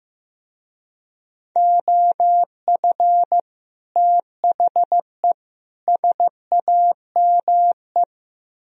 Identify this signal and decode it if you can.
Morse code, heard as a keyed tone, about 16 wpm, 700 Hz. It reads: OF THE SAME
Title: Speed-Racing - Sets of 3 Words 15wpm